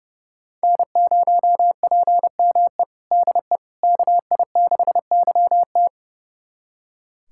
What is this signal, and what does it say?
Morse code, heard as a keyed tone, about 32 wpm, 700 Hz. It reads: N0PME DE KI6YT
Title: USA Callsign DE Callsign - 5 Characters - 30wpm